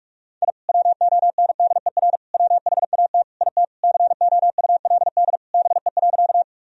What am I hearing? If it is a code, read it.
Morse code, heard as a keyed tone, about 42 wpm, 700 Hz. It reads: I WONDER WHAT IT COULD BE.
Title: Sentences from Top 1000 Words - Review Part 3 - 45wpm